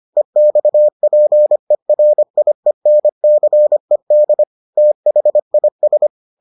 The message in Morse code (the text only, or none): EXPERIENCED THIS